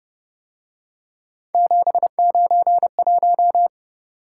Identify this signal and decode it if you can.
Morse code, heard as a keyed tone, about 29 wpm, 700 Hz. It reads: 791
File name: Speed-Racing - Three Letter-Number Combinations - 25wpm